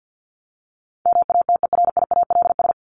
Morse code, heard as a keyed tone, about 50 wpm, 700 Hz. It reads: MATERIALS
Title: ES:S Suffix - Top 100 Words with Base Words 50wpm